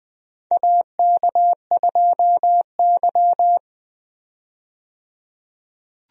Morse code, heard as a keyed tone, about 21 wpm, 700 Hz. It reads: AK2Y
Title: USA Call Signs - 4 Characters - 20wpm